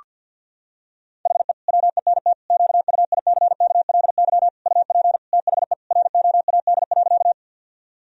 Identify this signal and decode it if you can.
Morse code, heard as a keyed tone, about 52 wpm, 700 Hz. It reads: HE WENT QUICKLY UP THE ROAD.
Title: Sentences from Top 1000 Words - Review Part 2 - 50wpm